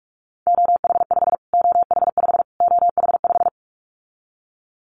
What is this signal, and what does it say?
Morse code, heard as a keyed tone, about 44 wpm, 700 Hz. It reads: OH5 OH5 OH5
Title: Callsign Prefixes for Top 25 DXCC - Repeated 3x - 45wpm